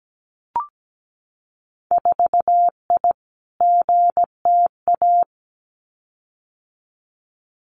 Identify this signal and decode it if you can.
Morse code, heard as a keyed tone, about 17 wpm, 700 Hz. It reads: 4I GTA
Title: Winter Field Day Exchanges Encore 17wpm